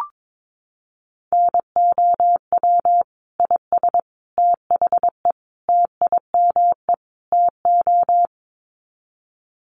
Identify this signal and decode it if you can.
Morse code, heard as a keyed tone, about 22 wpm, 700 Hz. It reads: NOW IS THE TIME TO
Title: Sets of 5 Words Encore2 22wpm